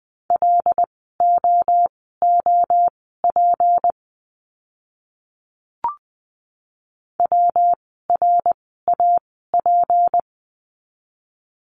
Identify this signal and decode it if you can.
Morse code, heard as a keyed tone, about 20 wpm, spaced as for 15 wpm, 700 Hz. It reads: LOOP WRAP
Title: 153 Lesson 25f - Letter p - Words 20wpm